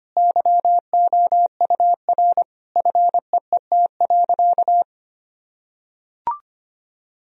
Morse code, heard as a keyed tone, about 25 wpm, 700 Hz. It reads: YOUR FEET.
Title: Sentences from Top 1000 Words - Review Part 1 - 25wpm